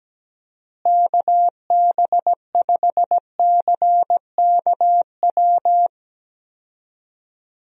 Morse code, heard as a keyed tone, about 17 wpm, 700 Hz. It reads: KB5CKW
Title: USA Call Signs - 6 Characters - Repeated 2x - 17wpm